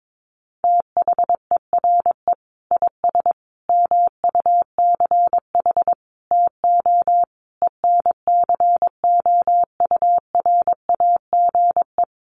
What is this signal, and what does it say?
Morse code, heard as a keyed tone, about 22 wpm, 700 Hz. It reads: THERE IS MUCH TO ENCOURAGE
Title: Sentences from Top 1400 Words 22wpm